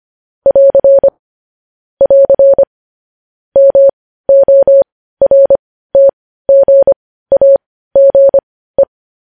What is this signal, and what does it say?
Morse code, heard as a keyed tone, about 25 wpm, spaced as for 15 wpm, 550 Hz. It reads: <AR> <AR> MORTGAGE
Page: news-headlines-in-morse-code-at-15-wpm